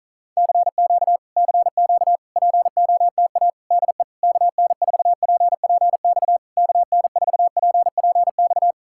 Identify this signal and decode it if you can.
Morse code, heard as a keyed tone, about 46 wpm, 700 Hz. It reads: CQ CQ POTA DE KN4PPX KN4PPX